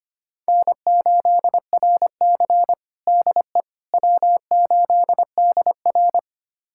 Morse code, heard as a keyed tone, about 25 wpm, 700 Hz. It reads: N8RC DE W8DR
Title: USA Callsign DE Callsign - 4 Characters - 25wpm